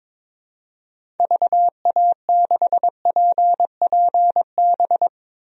Morse code, heard as a keyed tone, about 22 wpm, 700 Hz. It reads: VA6PPB
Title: DXCC-100 Call Signs - 6 Characters - Repeated 2x - 22wpm